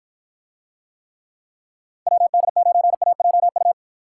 Morse code, heard as a keyed tone, about 49 wpm, 700 Hz. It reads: WD9AJU